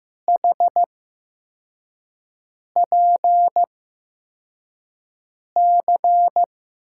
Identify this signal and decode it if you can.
Morse code, heard as a keyed tone, about 15 wpm, spaced as for 4 wpm, 700 Hz. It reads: HPC
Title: USA Call Signs - 8x Character Spacing - 15wpm